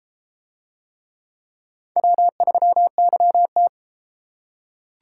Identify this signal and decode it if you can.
Morse code, heard as a keyed tone, about 33 wpm, 700 Hz. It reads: W3YT